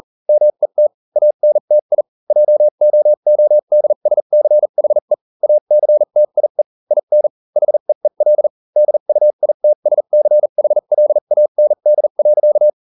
Morse code, heard as a keyed tone, about 40 wpm, spaced as for 35 wpm, 600 Hz. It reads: MET ANTI JOODSCHE ACTIE IN HEEL DUITSCHLAND.